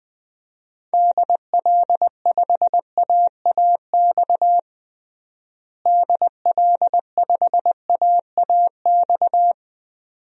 Morse code, heard as a keyed tone, about 20 wpm, 700 Hz. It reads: DL5AAX DL5AAX
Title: DXCC-100 Call Signs - 6 Characters - Repeated 2x - 20wpm